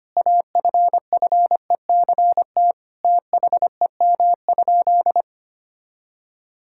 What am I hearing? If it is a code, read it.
Morse code, heard as a keyed tone, about 25 wpm, 700 Hz. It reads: AFFECT THEM?
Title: Sentences from Top 1000 Words - Review Part 6 - 25wpm